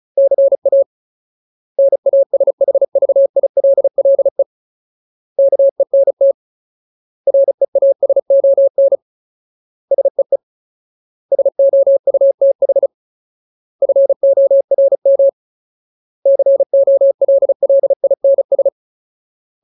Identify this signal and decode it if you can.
Morse code, heard as a keyed tone, about 34 wpm, 550 Hz. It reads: CA NASHVILLE KENT REASON SEE SOUTH FORM COLLINS
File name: Ultimate Word List / 35WPM Ultimate